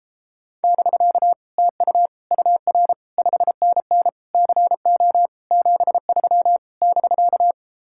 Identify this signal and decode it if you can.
Morse code, heard as a keyed tone, about 33 wpm, 700 Hz. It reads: <BK> TU UR 5NN CO 73 <BK>